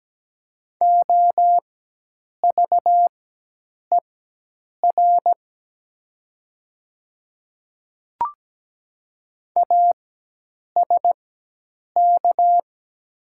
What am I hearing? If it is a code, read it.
Morse code, heard as a keyed tone, about 17 wpm, spaced as for 8 wpm, 700 Hz. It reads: OVER ASK
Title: 4x Character Spacing - Top 200 Words - 17wpm